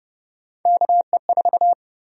Morse code, heard as a keyed tone, about 32 wpm, 700 Hz. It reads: KE4
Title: USA Call Sign Prefixes - Repeated 2x - 30wpm